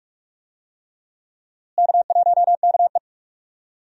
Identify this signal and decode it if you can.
Morse code, heard as a keyed tone, about 43 wpm, 700 Hz. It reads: K1KE
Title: USA Call Signs - 4 Characters - Repeated 3x - 45wpm